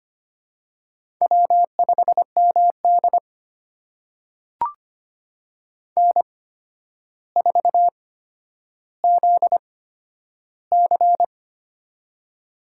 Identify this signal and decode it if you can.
Morse code, heard as a keyed tone, about 25 wpm, spaced as for 11 wpm, 700 Hz. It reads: W5MD N4ZC
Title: USA Call Signs - 8x Character Spacing - 25wpm